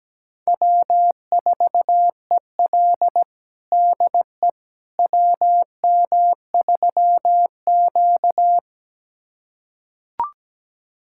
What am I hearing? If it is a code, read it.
Morse code, heard as a keyed tone, about 17 wpm, 700 Hz. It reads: W4EL DE WM3Q
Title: USA Callsign DE Callsign - 4 Characters - 17wpm